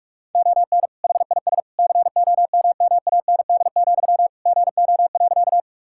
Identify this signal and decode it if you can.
Morse code, heard as a keyed tone, about 46 wpm, 700 Hz. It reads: ON HIS COMMAND, GO.
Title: Sentences from Top 1600 Words 45wpm